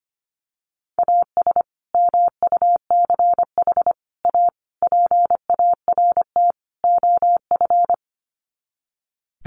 Morse code, heard as a keyed tone, about 25 wpm, 700 Hz. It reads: AS MUCH A PART OF
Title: Sets of 5 Words Encore2 25wpm